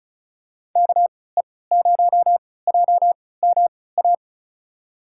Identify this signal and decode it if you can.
Morse code, heard as a keyed tone, about 36 wpm, spaced as for 20 wpm, 700 Hz. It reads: KE0JMA
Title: USA Call Signs - 6 Characters - 3x Character Spacing - 35wpm